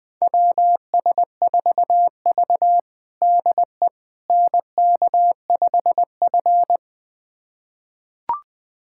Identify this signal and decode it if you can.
Morse code, heard as a keyed tone, about 20 wpm, 700 Hz. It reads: WS4V DE NK5F
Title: USA Callsign DE Callsign - 4 Characters - 20wpm